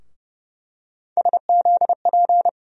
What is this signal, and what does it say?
Morse code, heard as a keyed tone, about 30 wpm, 700 Hz. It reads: SZP
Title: USA Call Sign Postfixes - 3 Letters 30wpm